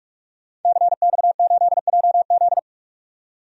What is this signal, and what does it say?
Morse code, heard as a keyed tone, about 46 wpm, 700 Hz. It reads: CX8JZ
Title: DXCC-100 Call Signs - 5 Characters - 45wpm